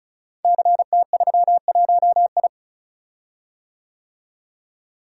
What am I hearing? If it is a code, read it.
Morse code, heard as a keyed tone, about 36 wpm, 690 Hz. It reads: CT31I
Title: Code Groups - 5 Letter:Numbers - 35wpm